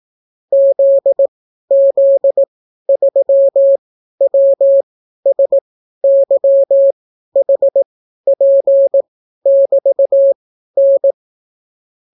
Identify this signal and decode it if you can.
Morse code, heard as a keyed tone, about 18 wpm, spaced as for 12 wpm, 550 Hz. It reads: ZZ3WSYHP<BT>N